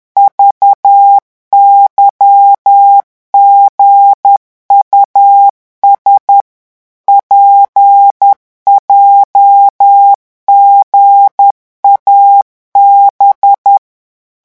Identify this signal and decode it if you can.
Morse code, heard as a keyed tone, about 10 wpm, 800 Hz. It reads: VYGUS PJGAB